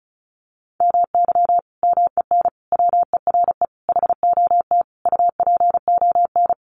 Morse code, heard as a keyed tone, about 34 wpm, 700 Hz. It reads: MY MEN WERE HOT UPON
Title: Sentences from Top 1100 Words 35wpm